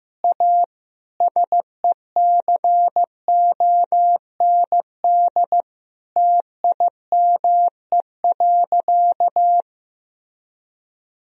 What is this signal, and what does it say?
Morse code, heard as a keyed tone, about 16 wpm, 700 Hz. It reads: A SECOND TIME.
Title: Sentences from Top 1000 Words - Review Part 1 - 15wpm